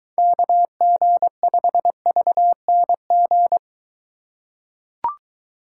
Morse code, heard as a keyed tone, about 23 wpm, 700 Hz. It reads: KG5VNG